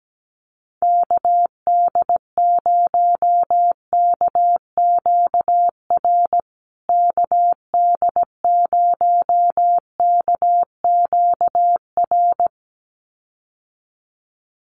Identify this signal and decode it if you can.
Morse code, heard as a keyed tone, about 17 wpm, 700 Hz. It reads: KD0KQR KD0KQR
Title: USA Call Signs Repeated 2x 17wpm